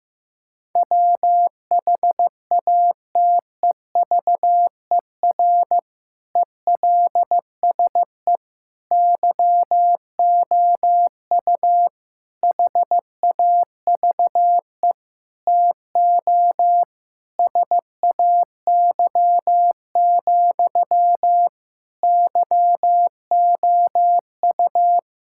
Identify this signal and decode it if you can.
Morse code, heard as a keyed tone, about 15 wpm, 700 Hz. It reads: WHATEVER ELSE YOU HAVE TO SAY, YOU
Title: Sentences from Top 1600 Words 15wpm